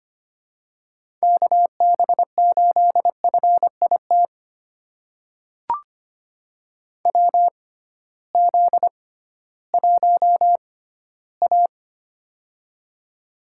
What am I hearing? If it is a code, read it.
Morse code, heard as a keyed tone, about 25 wpm, spaced as for 15 wpm, 700 Hz. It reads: KB8FIT WZ1A
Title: USA Call Signs - 6x Character Spacing - 25wpm